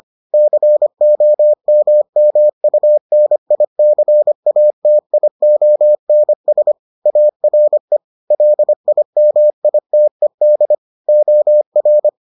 Morse code, heard as a keyed tone, about 25 wpm, 600 Hz. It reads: COMMUNICATIONS ARE LIMITED OR